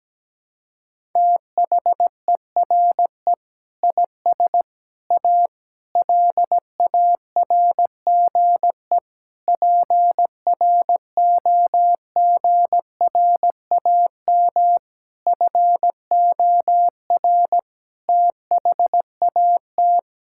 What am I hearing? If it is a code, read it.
Morse code, heard as a keyed tone, about 17 wpm, 700 Hz. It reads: THERE IS A LARGE PROGRAM FOR THAT
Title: Sentences from Top 1000 Words - Review Part 5 - 17wpm